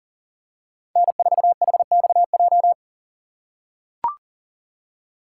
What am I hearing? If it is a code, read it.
Morse code, heard as a keyed tone, about 37 wpm, 700 Hz. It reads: N4HXJ
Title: USA Call Signs - 5 Characters - 3x Character Spacing - 40wpm